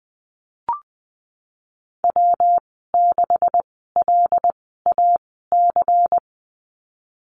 Morse code, heard as a keyed tone, about 19 wpm, spaced as for 14 wpm, 700 Hz. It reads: W6LAC
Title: 179 Lesson 29g - Slash - Call Signs 20wpm